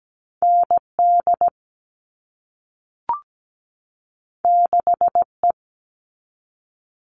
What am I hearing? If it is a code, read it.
Morse code, heard as a keyed tone, about 17 wpm, 700 Hz. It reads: ND 6E
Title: All Cut Numbers - 2-Digit 17wpm